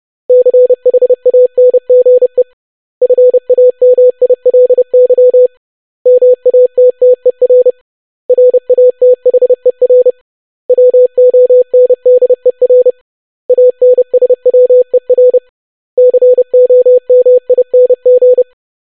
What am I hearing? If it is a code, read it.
Morse code, heard as a keyed tone, about 30 wpm, 500 Hz. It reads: CHANGE FAMILY MATTER RATHER WONDER ANSWER COMING
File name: basic500-25wpm